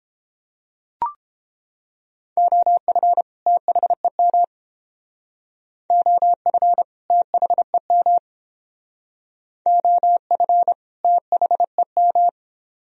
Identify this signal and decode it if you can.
Morse code, heard as a keyed tone, about 30 wpm, 700 Hz. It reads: OF THEM OF THEM OF THEM